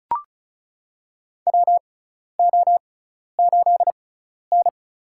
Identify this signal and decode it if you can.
Morse code, heard as a keyed tone, about 34 wpm, spaced as for 12 wpm, 700 Hz. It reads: WO8N
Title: USA Call Signs - 4 Characters - 6x Character Spacing - 35wpm